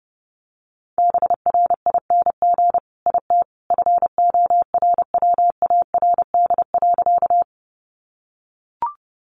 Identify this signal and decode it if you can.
Morse code, heard as a keyed tone, about 30 wpm, 700 Hz. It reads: BRING IT FORWARD.
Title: Sentences from Top 1300 Words 30wpm